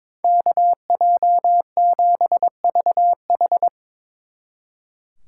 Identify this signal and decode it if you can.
Morse code, heard as a keyed tone, about 24 wpm, 700 Hz. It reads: KJ7VH
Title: Speed Racing - USA 2x2 Call Signs 22wpm